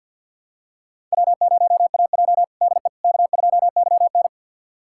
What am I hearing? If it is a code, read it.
Morse code, heard as a keyed tone, about 46 wpm, 700 Hz. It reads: W0AJ DE K2YN
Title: USA Callsign DE Callsign - 4 Characters - 50wpm